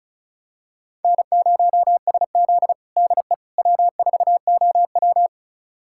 Morse code, heard as a keyed tone, about 36 wpm, 700 Hz. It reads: N0SZ DE W4OW
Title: USA Callsign DE Callsign - 4 Characters - 35wpm